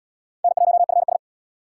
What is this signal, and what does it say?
Morse code, heard as a keyed tone, about 67 wpm, 700 Hz. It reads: N1FS